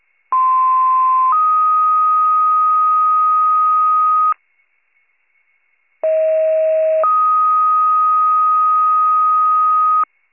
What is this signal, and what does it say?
Two-tone page, heard as a one-second tone, then a three-second tone, about 1040 then 1250 Hz, about 640 then 1250 Hz. It is heard from 0.3 to 10.0 s.